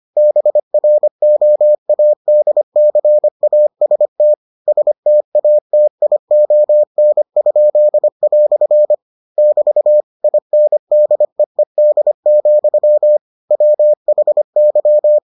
Morse code, heard as a keyed tone, about 25 wpm, 600 Hz. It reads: BROADCAST STATION?" <BT> INDEED, WHY